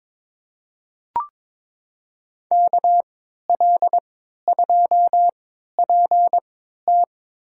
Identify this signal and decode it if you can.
Morse code, heard as a keyed tone, about 22 wpm, spaced as for 12 wpm, 700 Hz. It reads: KL2PT
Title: USA Call Signs - 5 Characters - 3x Character Spacing - 22wpm